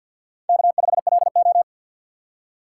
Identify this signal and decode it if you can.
Morse code, heard as a keyed tone, about 48 wpm, 700 Hz. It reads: K5LO
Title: USA Call Signs - 4 Characters - 6x Character Spacing - 50wpm